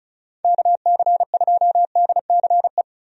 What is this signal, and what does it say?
Morse code, heard as a keyed tone, about 36 wpm, 700 Hz. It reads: KC2DCE
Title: USA Call Signs - 6 Characters - Repeated 3x - 35wpm